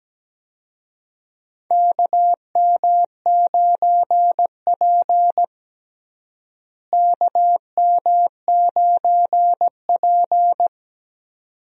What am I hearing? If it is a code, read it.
Morse code, heard as a keyed tone, about 17 wpm, 700 Hz. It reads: KM9P KM9P
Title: USA Call Signs - 4 Characters - Repeated 2x - 17wpm